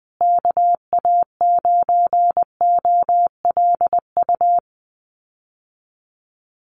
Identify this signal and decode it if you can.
Morse code, heard as a keyed tone, about 19 wpm, 700 Hz. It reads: KA9OLU
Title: USA Call Signs Repeated 2x 20wpm